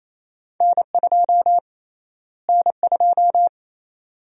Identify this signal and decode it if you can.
Morse code, heard as a keyed tone, about 28 wpm, 700 Hz. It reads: N2 N2
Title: USA Call Sign Prefixes - Repeated 2x - 28wpm